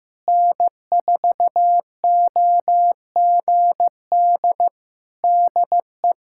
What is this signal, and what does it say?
Morse code, heard as a keyed tone, about 15 wpm, 700 Hz. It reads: N4OGD DE
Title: K1USN SST QSO Practice 15wpm